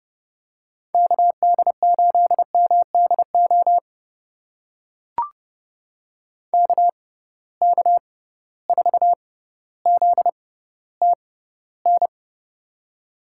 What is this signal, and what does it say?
Morse code, heard as a keyed tone, about 30 wpm, spaced as for 15 wpm, 700 Hz. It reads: KD8MDO KK4ZTN
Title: USA Call Signs - 6 Characters - 6x Character Spacing - 30wpm